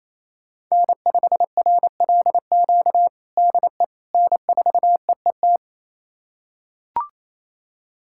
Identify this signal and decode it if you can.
Morse code, heard as a keyed tone, about 28 wpm, 700 Hz. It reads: N5RLQ DE N4EET
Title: USA Callsign DE Callsign - 5 Characters - 28wpm